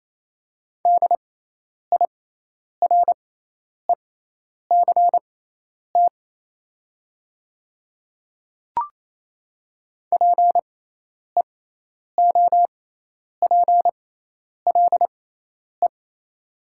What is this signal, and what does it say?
Morse code, heard as a keyed tone, about 28 wpm, spaced as for 10 wpm, 700 Hz. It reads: DIRECT PEOPLE
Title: 6x Character Spacing - Top 500 Words - 28wpm